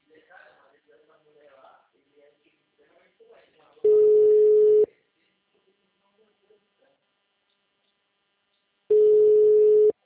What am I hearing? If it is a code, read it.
Ringback tone (Europe, 425 Hz), with speech.